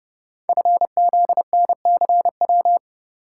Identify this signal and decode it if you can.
Morse code, heard as a keyed tone, about 30 wpm, 700 Hz. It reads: FZNCW
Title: Code Groups - 5 Letters - 30wpm